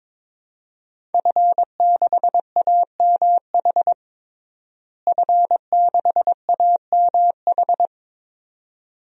Morse code, heard as a keyed tone, about 22 wpm, 700 Hz. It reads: F6AMH F6AMH
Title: DXCC-100 Call Signs - 5 Characters - Repeated 2x - 22wpm